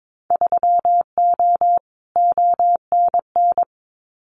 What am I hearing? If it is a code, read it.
Morse code, heard as a keyed tone, about 22 wpm, 700 Hz. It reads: 3O ONN
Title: Winter Field Day Exchanges 22wpm